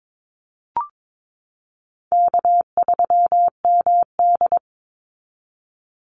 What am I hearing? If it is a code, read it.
Morse code, heard as a keyed tone, about 22 wpm, 700 Hz. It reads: K3MD
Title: Easy Call Signs Repeated 3x - 22wpm